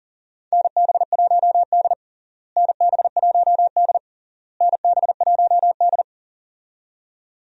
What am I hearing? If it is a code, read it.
Morse code, heard as a keyed tone, about 40 wpm, 700 Hz. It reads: NB1D NB1D NB1D
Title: USA Call Signs - 4 Characters - Repeated 3x - 40wpm